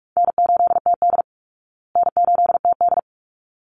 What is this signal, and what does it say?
Morse code, heard as a keyed tone, about 44 wpm, 700 Hz. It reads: N8TD N8TD
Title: Easy Call Signs Repeated 2x - 45wpm